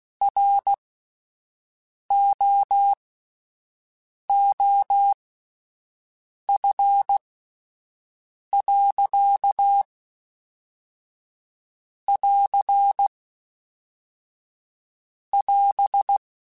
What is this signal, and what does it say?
Morse code, heard as a keyed tone, about 16 wpm, spaced as for 5 wpm, 800 Hz. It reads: ROOF.<AR><AS>
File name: Spaced Characters / Spaced Characters-mp3s / Weather4 24 sp @10